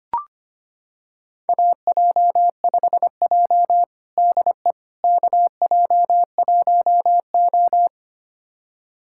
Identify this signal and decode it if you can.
Morse code, heard as a keyed tone, about 25 wpm, 700 Hz. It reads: AJ5J DE KJ1O
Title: USA Callsign DE Callsign - 4 Characters - 25wpm